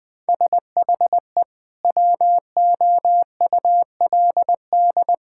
Morse code, heard as a keyed tone, about 20 wpm, 700 Hz. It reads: SHE WOULD
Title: Sentences with 4-Word Phrases - Spoken Beginning - 20wpm